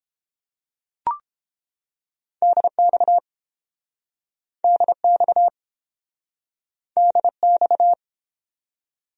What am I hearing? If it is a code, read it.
Morse code, heard as a keyed tone, about 32 wpm, 700 Hz. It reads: DX DX DX